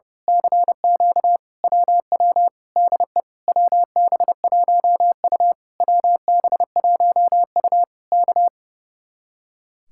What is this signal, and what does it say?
Morse code, heard as a keyed tone, about 30 wpm, 700 Hz. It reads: CQ WW DE WB1U WB1U K
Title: Speed-Racing Common QSO Phrases - 30wpm